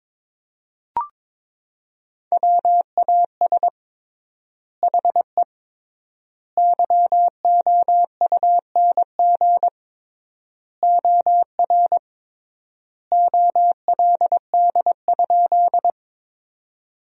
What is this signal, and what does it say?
Morse code, heard as a keyed tone, about 22 wpm, 700 Hz. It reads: WAS HE YOUNG OR OLD?
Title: 3x Word-Spacing - Sentences from Top 400 Words - 22wpm